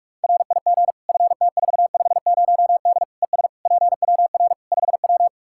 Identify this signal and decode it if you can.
Morse code, heard as a keyed tone, about 46 wpm, 700 Hz. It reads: RIG FT450D ES PWR 5W
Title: Practice QSO - Exchange 2 - Info - 45wpm